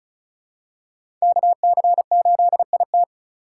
Morse code, heard as a keyed tone, about 36 wpm, 700 Hz. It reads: KC8IT
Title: USA Call Signs - 5 Characters - 8x Character Spacing - 35wpm